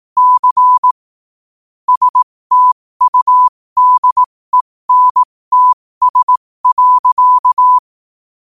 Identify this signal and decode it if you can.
Morse code, heard as a keyed tone, about 18 wpm, spaced as for 15 wpm, 1000 Hz. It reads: C STUDENTS.
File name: qotdfiles / 15wpm